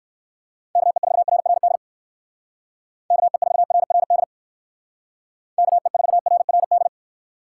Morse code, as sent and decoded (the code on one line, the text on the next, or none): -.- . ....- .-. .-. -.. / -.- . ....- .-. .-. -.. / -.- . ....- .-. .-. -..
KE4RRD KE4RRD KE4RRD